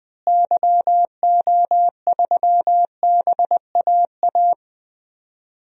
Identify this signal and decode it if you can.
Morse code, heard as a keyed tone, about 20 wpm, 700 Hz. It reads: YO3BAA
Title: DXCC-100 Call Signs - 6 Characters - Repeated 3x - 20wpm